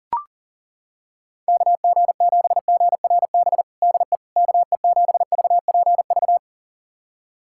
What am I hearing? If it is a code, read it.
Morse code, heard as a keyed tone, about 40 wpm, 700 Hz. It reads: KG7GRB DE KE7VPV
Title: USA Callsign DE Callsign - 6 Characters - 40wpm